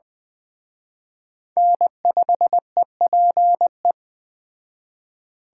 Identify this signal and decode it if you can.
Morse code, heard as a keyed tone, about 20 wpm, 700 Hz. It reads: N5EPE
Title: Speed Racing - USA 1x3 Call Signs 17wpm